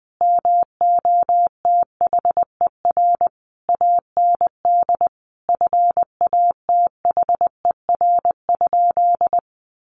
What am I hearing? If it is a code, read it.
Morse code, heard as a keyed tone, about 20 wpm, 700 Hz. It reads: MOTHER AND FATHER?
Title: Sentences from Top 1200 Words 20wpm